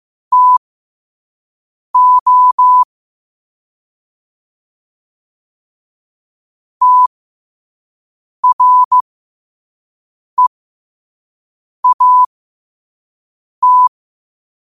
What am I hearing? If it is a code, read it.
Morse code, heard as a keyed tone, about 15 wpm, spaced as for 5 wpm, 1000 Hz. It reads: TO TREAT
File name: qotdfiles / 5wpm